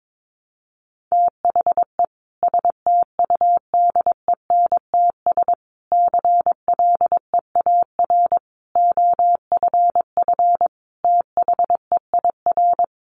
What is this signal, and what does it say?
Morse code, heard as a keyed tone, about 22 wpm, 700 Hz. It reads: THE STUDENTS CLEAR OFF THEIR
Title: Additional Easy Phrases 22wpm